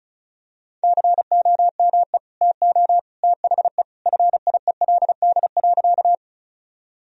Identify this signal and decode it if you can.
Morse code, heard as a keyed tone, about 36 wpm, 700 Hz. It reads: COME TO THE FIELD.
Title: Sentences from Top 1000 Words - Review Part 2 - 35wpm